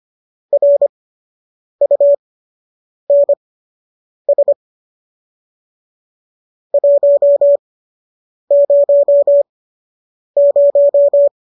Morse code, heard as a keyed tone, about 25 wpm, spaced as for 8 wpm, 570 Hz. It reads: RUNS 100
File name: qso003 08